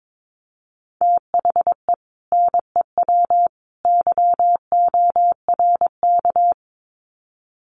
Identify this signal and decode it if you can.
Morse code, heard as a keyed tone, about 22 wpm, 700 Hz. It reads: THE NEW YORK
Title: Sets of 3 Words 22wpm